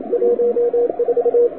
Morse code, heard as a keyed tone, about 36 wpm, 500 Hz, with radio noise. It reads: 14
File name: wwcw08 / k5zd wwcw08 1400z